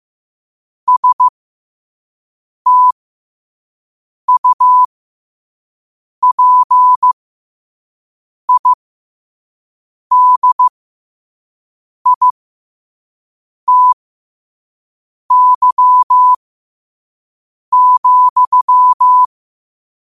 Morse code, heard as a keyed tone, about 15 wpm, spaced as for 5 wpm, 1000 Hz. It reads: STUPIDITY,
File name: qotdfiles / 5wpm